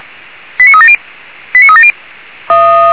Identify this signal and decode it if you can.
Five-tone selective call (ZVEI), heard as quick runs of five tones, tones of about 80 ms. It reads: 89370 89370